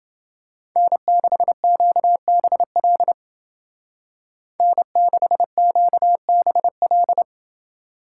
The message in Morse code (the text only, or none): N6QBL N6QBL